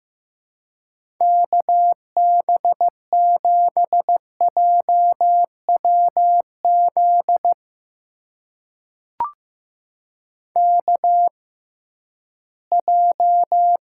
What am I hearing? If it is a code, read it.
Morse code, heard as a keyed tone, about 15 wpm, spaced as for 11 wpm, 700 Hz. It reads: KB7JWZ KJ
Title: USA Call Signs - 6 Characters - 6x Character Spacing - 15wpm